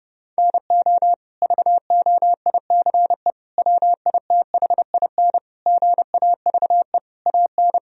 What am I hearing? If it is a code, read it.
Morse code, heard as a keyed tone, about 30 wpm, 700 Hz. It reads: NO VOICE WITHIN GAVE AN
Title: Sentences from Top 1000 Words - Review Part 4 - 30wpm